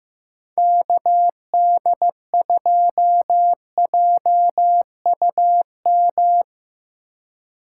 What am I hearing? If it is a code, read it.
Morse code, heard as a keyed tone, about 17 wpm, 700 Hz. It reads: KD2JUM
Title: Speed Racing - USA 2x3 Call Signs 15wpm